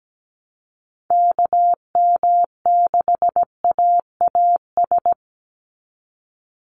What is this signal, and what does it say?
Morse code, heard as a keyed tone, about 17 wpm, 700 Hz. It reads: KM6AAS
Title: USA Call Signs 17wpm